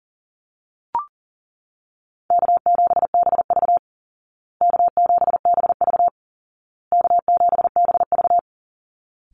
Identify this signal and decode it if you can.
Morse code, heard as a keyed tone, about 40 wpm, 700 Hz. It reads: K7BV K7BV K7BV
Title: Call Signs Repeated 2x - 40wpm